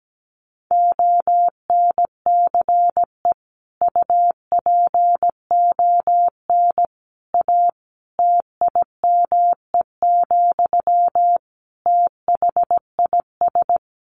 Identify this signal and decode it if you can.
Morse code, heard as a keyed tone, about 17 wpm, 700 Hz. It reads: ONCE UPON A TIME, THIS
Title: Sentences from Top 1500 Words 17wpm